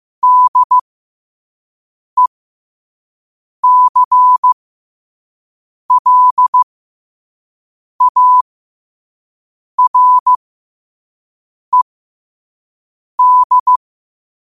Morse code, heard as a keyed tone, about 15 wpm, spaced as for 5 wpm, 1000 Hz. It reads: DECLARED
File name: qotdfiles / 5wpm